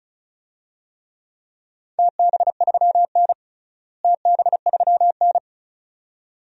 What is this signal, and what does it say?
Morse code, heard as a keyed tone, about 34 wpm, 700 Hz. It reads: TB3N TB3N
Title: DXCC-100 Call Signs - 4 Characters - Repeated 2x - 35wpm